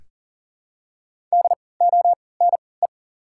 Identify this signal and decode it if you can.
Morse code, heard as a keyed tone, about 40 wpm, spaced as for 23 wpm, 700 Hz. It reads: DONE
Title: 3x Character Spacing - Top 500 Words - 40wpm